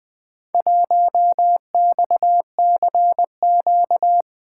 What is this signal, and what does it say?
Morse code, heard as a keyed tone, about 20 wpm, 700 Hz. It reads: 1XCQ
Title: Four Letter-Number Combinations - 20wpm